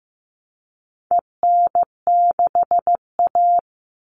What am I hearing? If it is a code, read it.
Morse code, heard as a keyed tone, about 15 wpm, 700 Hz. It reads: EN6A
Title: All Cut Numbers - 4-Digit 15wpm